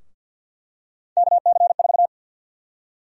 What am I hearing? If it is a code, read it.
Morse code, heard as a keyed tone, about 51 wpm, 700 Hz. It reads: KC4
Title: USA Call Sign Prefixes - 50wpm